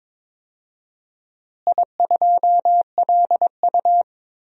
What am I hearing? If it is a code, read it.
Morse code, heard as a keyed tone, about 22 wpm, 700 Hz. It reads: I2LU
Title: DXCC-100 Call Signs - 4 Characters - Repeated 2x - 22wpm